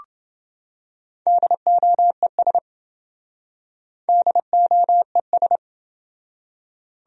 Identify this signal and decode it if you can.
Morse code, heard as a keyed tone, about 27 wpm, 700 Hz. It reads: DOES DOES